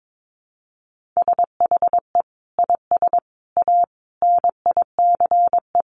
Morse code, heard as a keyed tone, about 22 wpm, 700 Hz. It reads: SHE IS A NICE
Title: Sentences from Top 1100 Words 22wpm